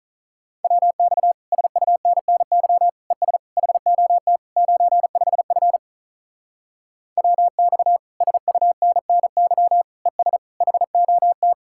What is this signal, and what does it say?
Morse code, heard as a keyed tone, about 42 wpm, 700 Hz. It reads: WX SUNNY ES HOT 95F WX SUNNY ES HOT
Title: Speed-Racing Common QSO Phrases - 35wpm